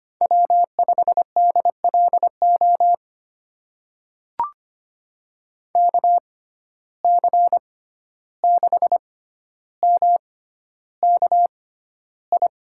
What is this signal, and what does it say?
Morse code, heard as a keyed tone, about 25 wpm, spaced as for 12 wpm, 700 Hz. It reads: W5DLO KC6MKI
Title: USA Call Signs - 6x Character Spacing - 25wpm